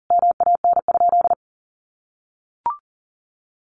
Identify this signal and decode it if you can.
Morse code, heard as a keyed tone, about 37 wpm, 700 Hz. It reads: MAN?
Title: Sentences from Top 1500 Words 40wpm